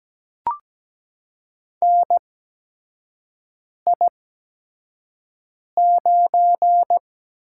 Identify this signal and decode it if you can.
Morse code, heard as a keyed tone, about 17 wpm, spaced as for 5 wpm, 700 Hz. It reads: NI9
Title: USA Call Signs - 5 Characters - 8x Character Spacing - 17wpm